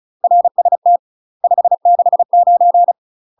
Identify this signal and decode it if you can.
Morse code, heard as a keyed tone, about 34 wpm, 700 Hz. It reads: RST 569
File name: UK MASTER QSOs TRAINING / 35WPM / UK MASTER QSOs 0014